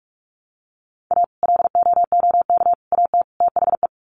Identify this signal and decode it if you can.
Morse code, heard as a keyed tone, about 44 wpm, 700 Hz. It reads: A LOOK AT THE
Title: Sets of 4 Words 45wpm